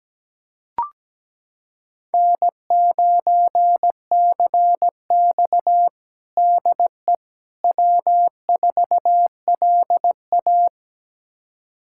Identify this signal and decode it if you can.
Morse code, heard as a keyed tone, about 17 wpm, 700 Hz. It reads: N9CX DE W4LA
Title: USA Callsign DE Callsign - 4 Characters - 17wpm